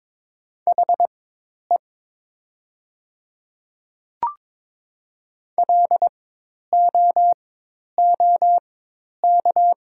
Morse code, heard as a keyed tone, about 22 wpm, spaced as for 10 wpm, 700 Hz. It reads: HE LOOK
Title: 4x Character Spacing - Top 300 Words - 22wpm